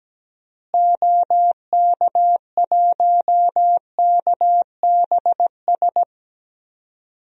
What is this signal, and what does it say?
Morse code, heard as a keyed tone, about 17 wpm, 700 Hz. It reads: OK1KBS
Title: DXCC-100 Call Signs - 6 Characters - Repeated 2x - 17wpm